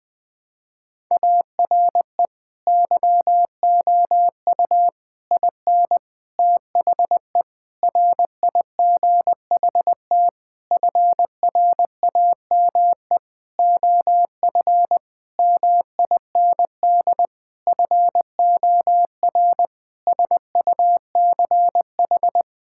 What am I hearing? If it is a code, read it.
Morse code, heard as a keyed tone, about 20 wpm, 700 Hz. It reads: ARE YOU IN THE RIGHT FRAME OF MIND FOR SUCH
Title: Sentences from Top 1000 Words - Review Part 6 - 20wpm